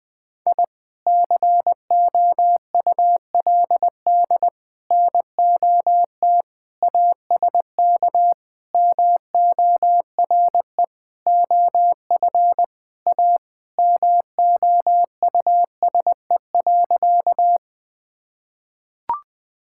Morse code, heard as a keyed tone, about 20 wpm, 700 Hz. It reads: I COULD NOT ASK MORE OF A MOUSE.
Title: Sentences from Top 1000 Words - Review Part 5 - 20wpm